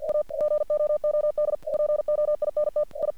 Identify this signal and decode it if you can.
Morse code, heard as a keyed tone, about 45 wpm, 600 Hz. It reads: A GOOD POINT I
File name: archived / CRSnet